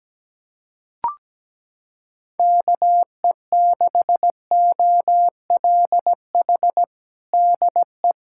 Morse code, heard as a keyed tone, about 17 wpm, 700 Hz. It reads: KE6OLH DE
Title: USA Callsign DE Callsign - 6 Characters - 17wpm